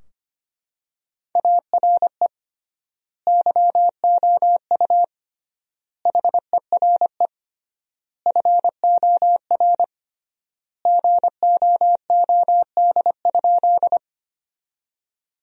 Morse code, heard as a keyed tone, about 25 wpm, 700 Hz. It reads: ARE YOU HERE FOR GOOD?
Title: 3x Word-Spacing - Sentences from Top 200 Words - 25wpm